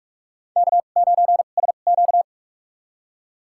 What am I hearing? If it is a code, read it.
Morse code, heard as a keyed tone, about 46 wpm, spaced as for 33 wpm, 700 Hz. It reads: K9SQ
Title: USA Call Signs - 4 Characters - 2x Character Spacing - 45wpm